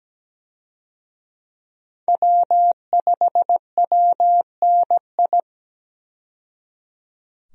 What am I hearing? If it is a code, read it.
Morse code, heard as a keyed tone, about 17 wpm, 700 Hz. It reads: W5WNI
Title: USA Call Signs - 5 Characters - 17wpm